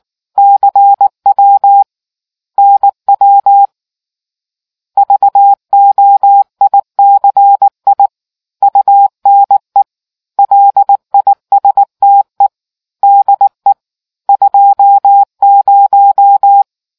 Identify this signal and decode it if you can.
Morse code, heard as a keyed tone, about 19 wpm, 800 Hz. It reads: CW NW VOICI UNE LISTE DE 20